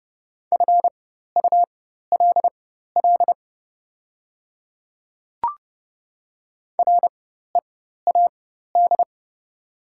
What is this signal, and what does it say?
Morse code, heard as a keyed tone, about 29 wpm, spaced as for 14 wpm, 700 Hz. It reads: FULL READ
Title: 4x Character Spacing - Top 500 Words - 30wpm